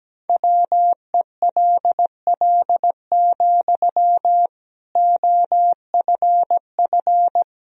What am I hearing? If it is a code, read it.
Morse code, heard as a keyed tone, about 17 wpm, 700 Hz. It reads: WELL, OFF
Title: Sentences from Top 1000 Words - Review Part 3 - 17wpm